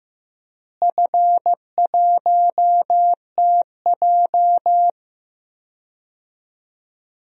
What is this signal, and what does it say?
Morse code, heard as a keyed tone, about 15 wpm, 700 Hz. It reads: F1TJ
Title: DXCC-100 Call Signs - 4 Characters - Repeated 3x - 15wpm